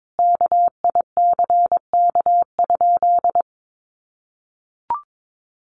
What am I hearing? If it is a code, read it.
Morse code, heard as a keyed tone, about 22 wpm, 700 Hz. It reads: KICK?
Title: Sentences from Top 1300 Words 22wpm